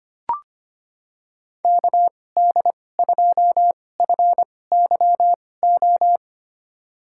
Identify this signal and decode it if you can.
Morse code, heard as a keyed tone, about 25 wpm, spaced as for 18 wpm, 700 Hz. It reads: KD2FYO
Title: USA Call Signs - 6 Characters - 2x Character Spacing - 25wpm